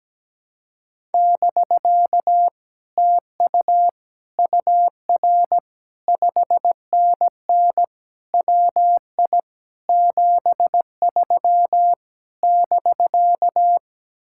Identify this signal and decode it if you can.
Morse code, heard as a keyed tone, about 17 wpm, 700 Hz. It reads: <BK> TU UR 5NN WI 73 <BK>
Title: Speed-Racing Common QSO Phrases - 17wpm